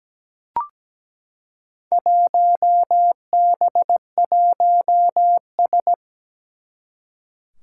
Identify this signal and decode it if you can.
Morse code, heard as a keyed tone, about 17 wpm, 700 Hz. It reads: 1B1S
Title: Four Challenging Letter-Number Combinations 17wpm